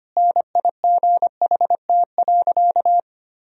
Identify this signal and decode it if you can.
Morse code, heard as a keyed tone, about 25 wpm, 700 Hz. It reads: NIGHT.
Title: Sentences from Top 1000 Words - Review Part 1 - 25wpm